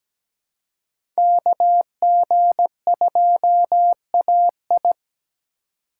Morse code, heard as a keyed tone, about 17 wpm, 700 Hz. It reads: KG2AI
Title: USA Call Signs - 5 Characters - 3x Character Spacing - 17wpm